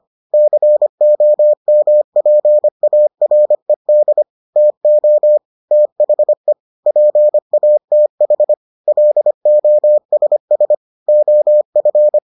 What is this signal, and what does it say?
Morse code, heard as a keyed tone, about 25 wpm, 600 Hz. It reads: COMPARED TO THE PATH LOSS OF